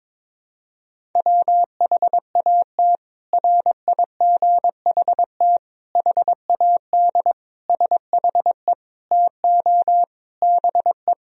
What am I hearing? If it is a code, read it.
Morse code, heard as a keyed tone, about 22 wpm, 700 Hz. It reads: WHAT RIGHT HAD SHE TO BE
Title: Sentences from Top 1000 Words - Review Part 1 - 22wpm